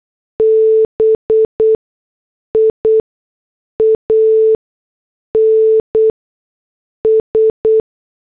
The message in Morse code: BIANS